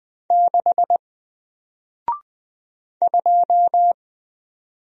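Morse code, heard as a keyed tone, about 20 wpm, 700 Hz. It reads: 6 2
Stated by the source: Single letter and number Morse drill at 20 wpm, from N4LCD.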